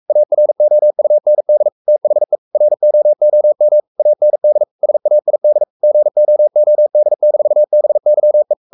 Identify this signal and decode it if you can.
Morse code, heard as a keyed tone, about 44 wpm, 600 Hz. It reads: AROUND THE ROOM AND SAID GOOD-BYE
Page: Romanen Never af Ken Follet på engelsk.